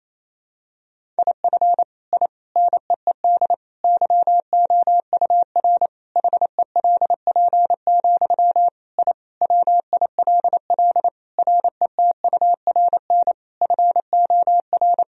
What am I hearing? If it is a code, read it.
Morse code, heard as a keyed tone, about 28 wpm, 700 Hz. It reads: IF I NEED YOUR HELP, I WILL RETURN FOR
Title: Sentences from Top 1000 Words - Review Part 1 - 28wpm